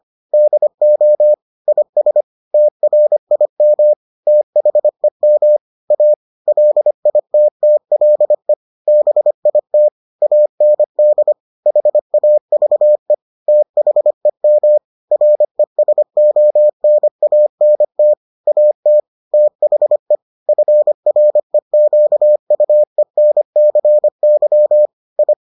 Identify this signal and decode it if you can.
Morse code, heard as a keyed tone, about 25 wpm, 600 Hz. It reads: DO IS TRIM THEM A LITTLE BIT AND HAVE THEM RESONANT AT THE FREQUENCY I